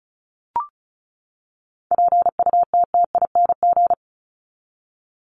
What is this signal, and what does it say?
Morse code, heard as a keyed tone, about 34 wpm, 700 Hz. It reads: PUTTING
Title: ING Suffix - Top 100 Words with Base Words 35wpm